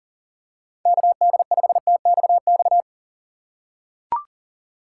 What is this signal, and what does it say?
Morse code, heard as a keyed tone, about 37 wpm, 700 Hz. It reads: KD5TXX
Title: K1USN SST QSO Practice - 40wpm